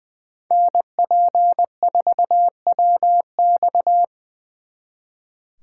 Morse code, heard as a keyed tone, about 23 wpm, 700 Hz. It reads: NP4WX
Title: Speed Racing - USA 2x2 Call Signs 20wpm